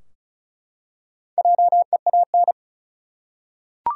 Morse code, heard as a keyed tone, about 36 wpm, 700 Hz. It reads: JEAN
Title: USA Common Names - Max 5 Letters - 35wpm